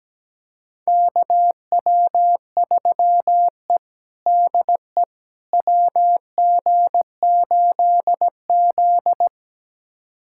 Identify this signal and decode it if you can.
Morse code, heard as a keyed tone, about 17 wpm, 700 Hz. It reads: KW3E DE WG8Z
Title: USA Callsign DE Callsign - 4 Characters - 17wpm